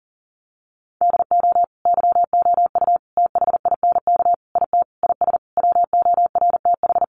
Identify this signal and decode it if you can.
Morse code, heard as a keyed tone, about 40 wpm, 700 Hz. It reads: DO YOU THINK IT IS WORTH
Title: Sentences from Top 800 Words 40wpm